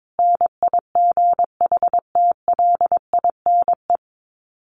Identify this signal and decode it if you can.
Morse code, heard as a keyed tone, about 22 wpm, 700 Hz. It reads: NIGHTLINE
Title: Sets of 5 Words 22wpm